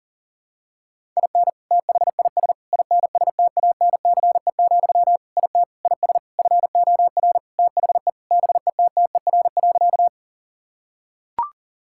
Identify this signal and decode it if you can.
Morse code, heard as a keyed tone, about 40 wpm, 700 Hz. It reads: IN THIS INSTANCE, IT IS FOR THE BETTER.
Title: Sentences from Top 1000 Words - Review Part 5 - 40wpm